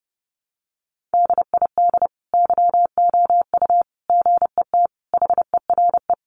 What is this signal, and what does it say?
Morse code, heard as a keyed tone, about 30 wpm, 700 Hz. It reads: DID YOU GET HERE
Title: Sentences from Top 700 Words 30wpm